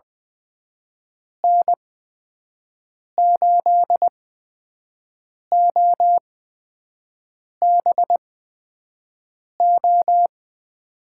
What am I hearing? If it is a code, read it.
Morse code, heard as a keyed tone, about 20 wpm, spaced as for 5 wpm, 700 Hz. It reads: N8OBO
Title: USA Call Signs - 8x Character Spacing - 20wpm